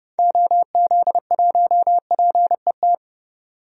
Morse code, heard as a keyed tone, about 30 wpm, 700 Hz. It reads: OZ1PET
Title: DXCC-100 Call Signs - 6 Characters - 30wpm